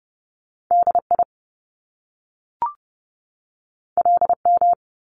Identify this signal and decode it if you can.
Morse code, heard as a keyed tone, about 30 wpm, 700 Hz. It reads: DI LM
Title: Top Two Letter Combinations in Words 30wpm